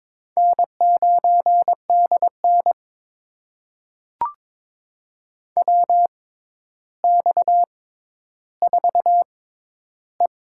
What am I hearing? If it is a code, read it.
Morse code, heard as a keyed tone, about 22 wpm, spaced as for 11 wpm, 700 Hz. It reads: N9DN WX4E
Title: USA Call Signs - 4 Characters - 6x Character Spacing - 22wpm